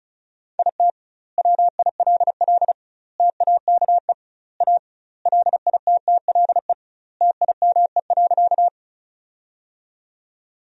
Morse code, heard as a keyed tone, about 36 wpm, 700 Hz. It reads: IT WILL TAKE A LITTLE TIME.
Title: 2x Word-Spacing - Sentences from Top 200 Words - 35wpm